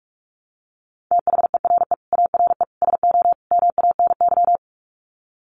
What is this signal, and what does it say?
Morse code, heard as a keyed tone, about 44 wpm, 700 Hz. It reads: THERE ARE SO MANY
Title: Sets of 4 Words 45wpm